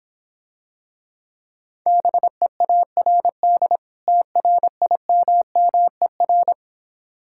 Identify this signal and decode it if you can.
Morse code, heard as a keyed tone, about 26 wpm, 700 Hz. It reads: BEARD TRIMMER